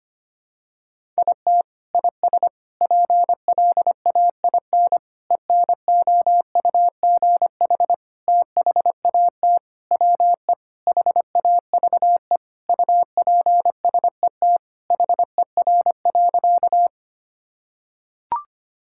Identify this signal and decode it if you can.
Morse code, heard as a keyed tone, about 25 wpm, 700 Hz. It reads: IT IS PLAIN ENOUGH THAT WE HAVE UPSET HER.
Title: Sentences from Top 1000 Words - Review Part 5 - 25wpm